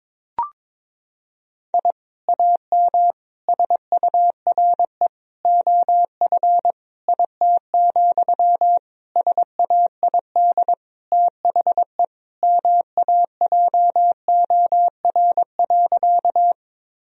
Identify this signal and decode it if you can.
Morse code, heard as a keyed tone, about 21 wpm, 700 Hz. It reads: I AM SURE OF IT, SAID THE MAJOR.
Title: Sentences from Top 1000 Words - Review Part 4 - 22wpm